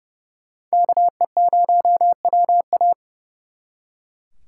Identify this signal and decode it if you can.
Morse code, heard as a keyed tone, about 37 wpm, 700 Hz. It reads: KE0WA
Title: Speed Racing - USA 2x2 Call Signs 30wpm